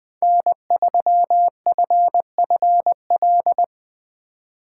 Morse code, heard as a keyed tone, about 23 wpm, 700 Hz. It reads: N3FFL
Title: Speed Racing - USA 1x3 Call Signs 20wpm